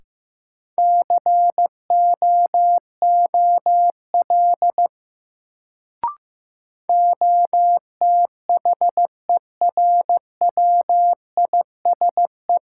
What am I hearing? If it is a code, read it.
Morse code, heard as a keyed tone, about 15 wpm, 700 Hz. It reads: COOL OTHERWISE
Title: Sentences from Top 1200 Words - New Word Drill - 15wpm